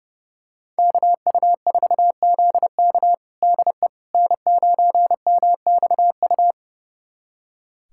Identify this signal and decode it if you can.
Morse code, heard as a keyed tone, about 30 wpm, 700 Hz. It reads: KU4ZK DE N9MXU
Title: USA Callsign DE Callsign - 5 Characters - 30wpm